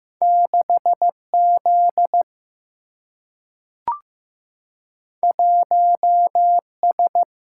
Morse code, heard as a keyed tone, about 15 wpm, 700 Hz. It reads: 6 Z 1 S
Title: Single-Letter Single-Number 15wpm